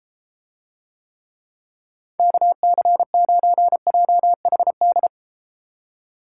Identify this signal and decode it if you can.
Morse code, heard as a keyed tone, about 32 wpm, 700 Hz. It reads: KC9JHD